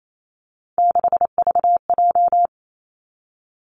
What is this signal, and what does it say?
Morse code, heard as a keyed tone, about 28 wpm, 700 Hz. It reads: 6VJ
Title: Three Challenging Letter-Number Combinations 28wpm